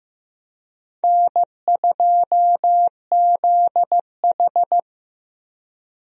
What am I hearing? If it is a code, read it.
Morse code, heard as a keyed tone, about 15 wpm, 700 Hz. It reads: N2ZH
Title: USA Call Signs - 4 Characters - Repeated 3x - 15wpm